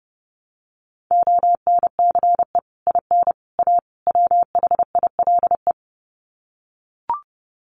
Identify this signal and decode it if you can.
Morse code, heard as a keyed tone, about 30 wpm, 700 Hz. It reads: ONCE IN A WHILE
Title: Sets of 4 Words 30wpm